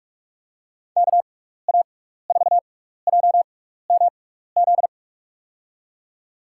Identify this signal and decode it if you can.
Morse code, heard as a keyed tone, about 42 wpm, spaced as for 15 wpm, 700 Hz. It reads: KA4JMZ
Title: USA Call Signs - 6 Characters - 6x Character Spacing - 45wpm